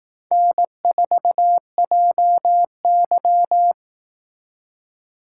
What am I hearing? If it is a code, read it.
Morse code, heard as a keyed tone, about 18 wpm, 700 Hz. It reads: N4JY
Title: Speed Racing - USA Call Signs 15wpm